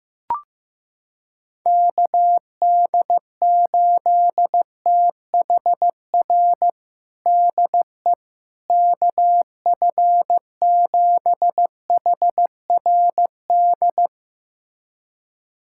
Morse code, heard as a keyed tone, about 15 wpm, 700 Hz. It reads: KD8THR DE KF7HRD
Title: USA Callsign DE Callsign - 6 Characters - 15wpm